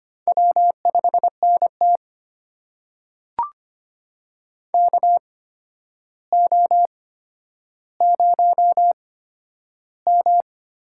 Morse code, heard as a keyed tone, about 25 wpm, spaced as for 11 wpm, 700 Hz. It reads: W5NT KO0M
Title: USA Call Signs - 4 Characters - 8x Character Spacing - 25wpm